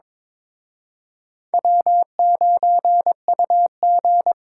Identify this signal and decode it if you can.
Morse code, heard as a keyed tone, about 26 wpm, 700 Hz. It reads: W9UG
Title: Speed Racing - USA Call Signs 22wpm